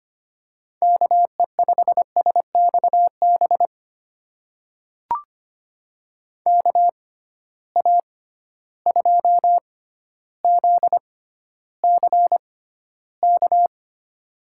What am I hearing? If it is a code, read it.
Morse code, heard as a keyed tone, about 25 wpm, spaced as for 13 wpm, 700 Hz. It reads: KE5SXB KA2ZCK
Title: USA Call Signs - 6 Characters - 6x Character Spacing - 25wpm